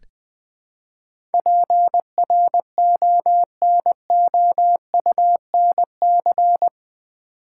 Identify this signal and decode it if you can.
Morse code, heard as a keyed tone, about 20 wpm, 700 Hz. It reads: PRONOUNC
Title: Build A Word - 4 Letters At A Time - 10-Letter Words - 20wpm